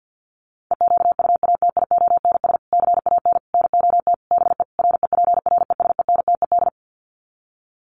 Morse code, heard as a keyed tone, about 50 wpm, 700 Hz. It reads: EQUATIONS CAN NOT BE REPRESENTED
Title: Sets of 5 Words 50wpm